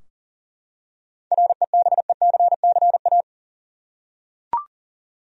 Morse code, heard as a keyed tone, about 41 wpm, 700 Hz. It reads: REBECCA
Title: USA Common Names - 40wpm